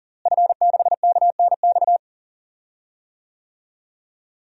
Morse code, heard as a keyed tone, about 40 wpm, 700 Hz. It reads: F6KNX
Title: DXCC-100 Call Signs - 5 Characters - Repeated 3x - 40wpm